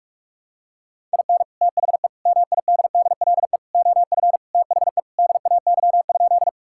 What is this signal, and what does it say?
Morse code, heard as a keyed tone, about 46 wpm, 700 Hz. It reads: IN THE MIDDLE OF THE DAY?
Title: Sentences from Top 1000 Words - Review Part 6 - 45wpm